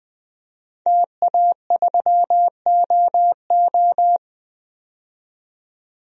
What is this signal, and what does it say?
Morse code, heard as a keyed tone, about 20 wpm, 700 Hz. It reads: TA3OO
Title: DXCC-100 Call Signs - 5 Characters - 20wpm